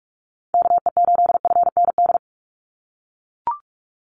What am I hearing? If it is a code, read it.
Morse code, heard as a keyed tone, about 44 wpm, 700 Hz. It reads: KE8FND
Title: USA Call Signs 45wpm